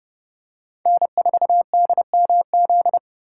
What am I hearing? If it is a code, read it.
Morse code, heard as a keyed tone, about 28 wpm, 700 Hz. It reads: N4DMZ